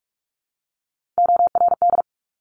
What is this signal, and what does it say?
Morse code, heard as a keyed tone, about 46 wpm, 700 Hz. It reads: ORD
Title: Top Three Letter Combinations in Words 45wpm